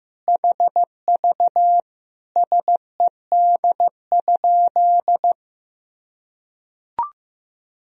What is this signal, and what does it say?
Morse code, heard as a keyed tone, about 15 wpm, 700 Hz. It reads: HV SED?
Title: Sentences from Top 100 Words and Abbreviations 15wpm